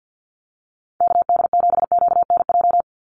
Morse code, heard as a keyed tone, about 50 wpm, 700 Hz. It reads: KD7QNJ
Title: USA Call Signs 50wpm